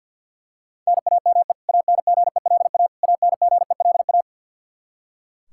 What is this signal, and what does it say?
Morse code, heard as a keyed tone, about 46 wpm, 700 Hz. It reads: NAME ANGELA ANGELA
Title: Practice QSO - Exchange 1 - Beginning - 50wpm